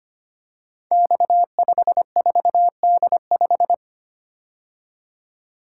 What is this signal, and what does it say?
Morse code, heard as a keyed tone, about 25 wpm, 700 Hz. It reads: X54D5
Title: Code Groups - 5 Letter:Numbers - 25wpm